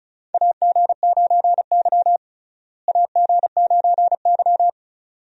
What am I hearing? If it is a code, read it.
Morse code, heard as a keyed tone, about 36 wpm, 700 Hz. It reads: AG9Y AG9Y
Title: USA Call Signs - 4 Characters - Repeated 2x - 35wpm